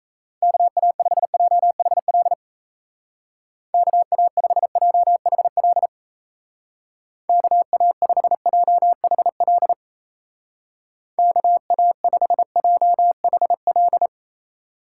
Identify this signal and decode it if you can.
Morse code, heard as a keyed tone, about 34 wpm, 700 Hz. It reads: KA5JHL KA5JHL KA5JHL KA5JHL
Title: Speed Racing - USA 2x3 Call Signs 28wpm